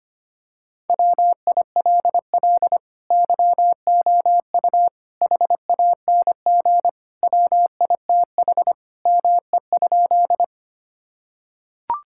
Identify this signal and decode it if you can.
Morse code, heard as a keyed tone, about 25 wpm, 700 Hz. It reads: WILL YOU HANG WITH ME?
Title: Sentences from Top 1600 Words 25wpm